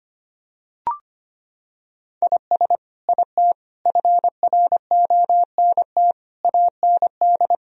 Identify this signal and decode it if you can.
Morse code, heard as a keyed tone, about 25 wpm, 700 Hz. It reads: IS IT FRONT AND
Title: Sentences from Top 1000 Words - Review Part 2 - 25wpm